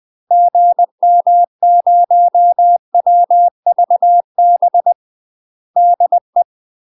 Morse code, heard as a keyed tone, about 20 wpm, 700 Hz. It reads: GM0WVB DE
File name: UK MASTER QSOs TRAINING / 20WPM / UK MASTER QSOs 0059